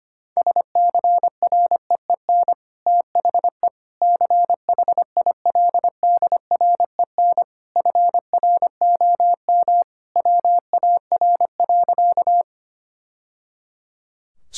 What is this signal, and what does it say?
Morse code, heard as a keyed tone, about 25 wpm, 700 Hz. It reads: SCREEN THE CHILDREN FROM WAR.
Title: Sentences from Top 1000 Words - Review Part 6 - 25wpm